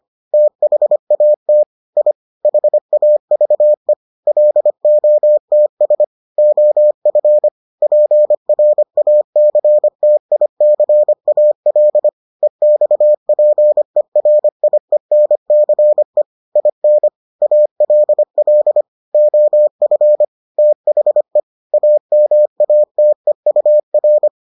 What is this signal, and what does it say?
Morse code, heard as a keyed tone, about 25 wpm, 600 Hz. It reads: THAT I HAVE LOTS OF PRACTICAL EXPERIENCE IN ALL OF THE AMATEUR